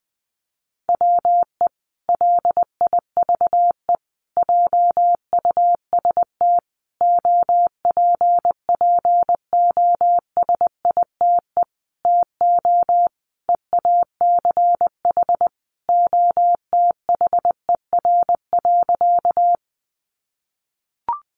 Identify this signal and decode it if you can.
Morse code, heard as a keyed tone, about 20 wpm, 700 Hz. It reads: WE LIVE JUST OPPOSITE TO EACH OTHER.
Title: Sentences from Top 1300 Words 20wpm